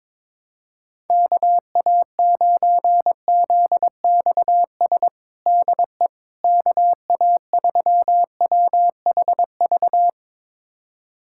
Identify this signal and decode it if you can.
Morse code, heard as a keyed tone, about 22 wpm, 700 Hz. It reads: KA9ZXS DE KA3WHV
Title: USA Callsign DE Callsign - 6 Characters - 22wpm